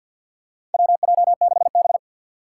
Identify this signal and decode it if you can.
Morse code, heard as a keyed tone, about 52 wpm, 700 Hz. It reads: WJ6B
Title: Four Challenging Letter-Number Combinations 50wpm